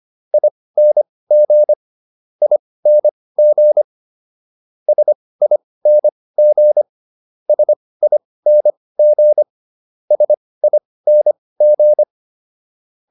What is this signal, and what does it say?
Morse code, heard as a keyed tone, about 25 wpm, spaced as for 18 wpm, 600 Hz. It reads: ING ING SING SING SING